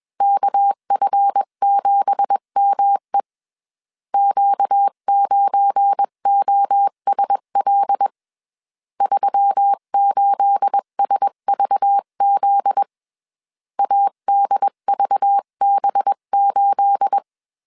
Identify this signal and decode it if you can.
Morse code, heard as a keyed tone, about 21 wpm, 800 Hz. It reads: KF7ME Q9OSL 38SVZ ADVB8
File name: CW / DET / 1 9 1500C S B 20wpm 2313